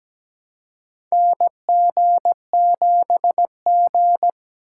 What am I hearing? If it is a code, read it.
Morse code, heard as a keyed tone, about 17 wpm, 700 Hz. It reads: NG7G
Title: USA Call Signs - 4 Characters - 2x Character Spacing - 17wpm